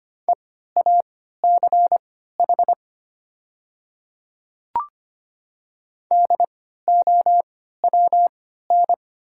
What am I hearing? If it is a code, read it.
Morse code, heard as a keyed tone, about 25 wpm, spaced as for 14 wpm, 690 Hz. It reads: EACH DOWN
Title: 3x Character Spacing - Top 100 Words - 25wpm